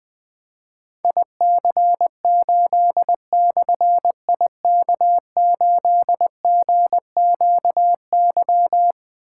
Morse code, heard as a keyed tone, about 20 wpm, 700 Hz. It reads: IC8/IK8GQY
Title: Difficult Call Signs 20wpm